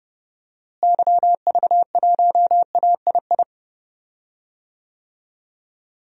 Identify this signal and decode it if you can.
Morse code, heard as a keyed tone, about 30 wpm, 700 Hz. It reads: YV1AII
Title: DXCC-100 Call Signs - 6 Characters - Repeated 2x - 30wpm